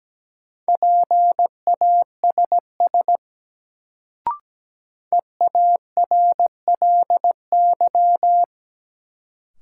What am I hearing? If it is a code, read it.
Morse code, heard as a keyed tone, about 17 wpm, 700 Hz. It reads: PASS EARLY
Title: Sentences from Top 400 Words - New Word Drill - 17wpm